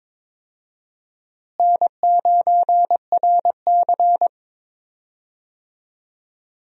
Morse code, heard as a keyed tone, about 22 wpm, 700 Hz. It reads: N9RC
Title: USA Call Signs - 4 Characters - Repeated 3x - 22wpm